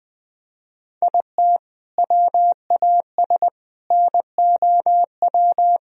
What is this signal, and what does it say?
Morse code, heard as a keyed tone, about 20 wpm, 700 Hz. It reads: IT WAS NOW
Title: Sentences from Top 200 Words and Abbreviations 20wpm